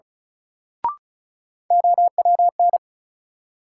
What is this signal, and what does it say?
Morse code, heard as a keyed tone, about 36 wpm, 700 Hz. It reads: OWN
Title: Sets of 2 Words - New Word Drill - 35wpm